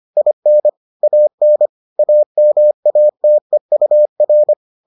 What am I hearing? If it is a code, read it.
Morse code, heard as a keyed tone, about 25 wpm, 600 Hz. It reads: IN AN AMATEUR